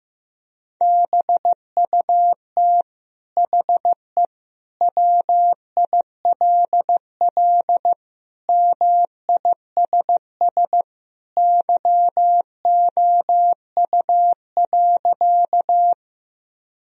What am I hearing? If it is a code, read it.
Morse code, heard as a keyed tone, about 15 wpm, 700 Hz. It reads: BUT HE WILL MISS YOU.
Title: Sentences from Top 1000 Words - Review Part 3 - 15wpm